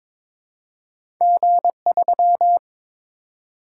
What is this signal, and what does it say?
Morse code, heard as a keyed tone, about 22 wpm, 700 Hz. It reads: G3
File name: practice / 22wpm / Two Letter-Number Combinations - 22wpm